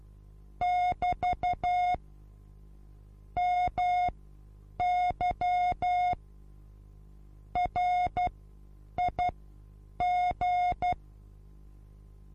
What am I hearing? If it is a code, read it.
Morse code, heard as a keyed tone, about 12 wpm, spaced as for 8 wpm, 720 Hz. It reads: <BT> MY RIG
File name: Onlinelehrgang / morsen / DJ4UF Morsen23